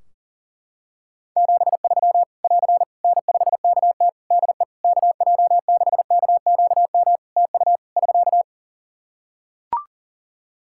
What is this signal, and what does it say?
Morse code, heard as a keyed tone, about 40 wpm, 700 Hz. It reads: 73 <AR> N5KT DE KJ6KQM TU <SK>
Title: Practice QSO - Exchange 4 - Ending - 40wpm